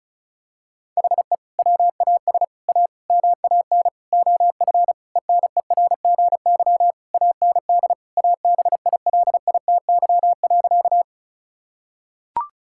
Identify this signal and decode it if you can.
Morse code, heard as a keyed tone, about 36 wpm, 700 Hz. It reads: HE WAS A MAN OF ENERGY AND ABILITY.
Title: Sentences from Top 1000 Words - Review Part 6 - 35wpm